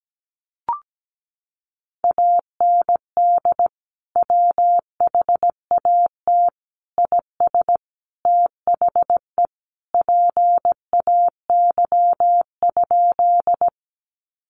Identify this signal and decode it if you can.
Morse code, heard as a keyed tone, about 17 wpm, 700 Hz. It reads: AND WHAT IS THE PAY?
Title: Sentences from Top 700 Words 17wpm